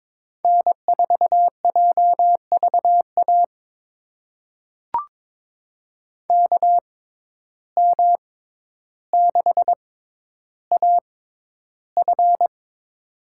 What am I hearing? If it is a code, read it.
Morse code, heard as a keyed tone, about 22 wpm, spaced as for 11 wpm, 700 Hz. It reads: N4JVA KM6AF
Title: USA Call Signs - 5 Characters - 6x Character Spacing - 22wpm